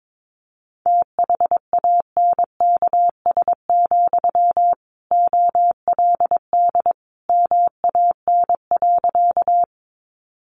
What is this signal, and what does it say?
Morse code, heard as a keyed tone, about 22 wpm, 700 Hz. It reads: THANKS, OLD MAN.
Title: Sentences from Top 600 Words 22wpm